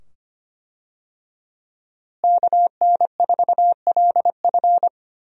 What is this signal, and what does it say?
Morse code, heard as a keyed tone, about 25 wpm, 700 Hz. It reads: KN4LF
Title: USA Call Signs - 5 Characters - Repeated 2x - 25wpm